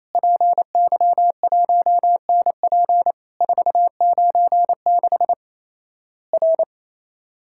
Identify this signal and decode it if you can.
Morse code, heard as a keyed tone, about 28 wpm, 700 Hz. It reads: PY1NP 496 R
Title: Worked All Europe DX Contest - QTC Exchange - 28wpm